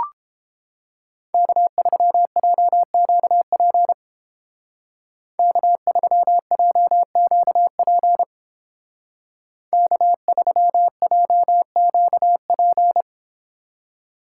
Morse code, heard as a keyed tone, about 29 wpm, 700 Hz. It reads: K3JQP K3JQP K3JQP